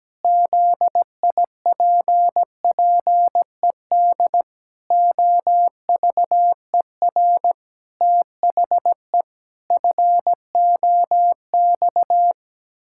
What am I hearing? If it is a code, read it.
Morse code, heard as a keyed tone, about 17 wpm, 700 Hz. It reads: ZIPPED OVER THE FOX
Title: Pangram Challenge v2 - 17wpm